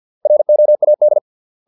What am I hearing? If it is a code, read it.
Morse code, heard as a keyed tone, about 50 wpm, 600 Hz. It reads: ROAD